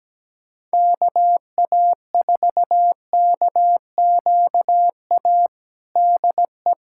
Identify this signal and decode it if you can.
Morse code, heard as a keyed tone, about 17 wpm, 700 Hz. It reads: KA4KQA DE
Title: USA Callsign DE Callsign 17wpm